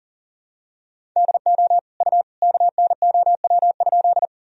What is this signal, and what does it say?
Morse code, heard as a keyed tone, about 39 wpm, 700 Hz. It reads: DO U KNOW?
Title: Sentences from Top 100 Words and Abbreviations 40wpm